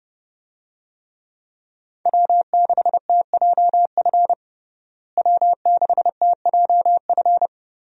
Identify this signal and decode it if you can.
Morse code, heard as a keyed tone, about 29 wpm, 700 Hz. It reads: W6TJF W6TJF
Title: USA Call Signs - 5 Characters - Repeated 3x - 30wpm